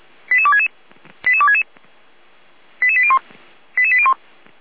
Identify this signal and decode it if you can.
Five-tone selective call (ZVEI), heard as quick runs of five tones, tones of about 70 ms. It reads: 88266 88266 88801 88801